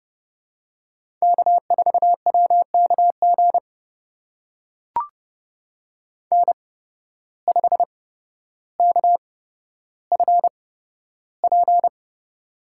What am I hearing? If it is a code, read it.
Morse code, heard as a keyed tone, about 30 wpm, spaced as for 13 wpm, 700 Hz. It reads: K4WKG N5KFP
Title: USA Call Signs - 5 Characters - 8x Character Spacing - 30wpm